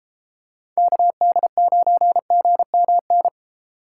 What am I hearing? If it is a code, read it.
Morse code, heard as a keyed tone, about 32 wpm, 700 Hz. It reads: KD9GMN